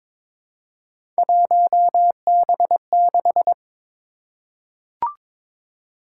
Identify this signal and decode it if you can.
Morse code, heard as a keyed tone, about 22 wpm, 700 Hz. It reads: 1B6
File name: practice / 22wpm / Three Letter-Number Combinations - 22wpm